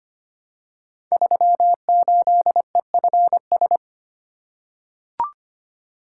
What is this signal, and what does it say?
Morse code, heard as a keyed tone, about 25 wpm, 700 Hz. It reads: 38EFS
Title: Code Groups - 5 Letter:Numbers - 25wpm